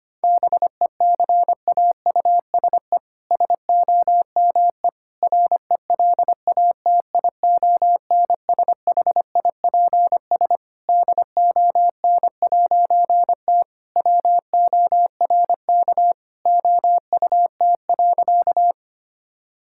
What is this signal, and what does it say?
Morse code, heard as a keyed tone, about 25 wpm, 700 Hz. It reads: BECAUSE SOME RELATIONSHIPS DON'T WORK OUT.
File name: practice / 25wpm / All-Age Appropriate Jokes - 25wpm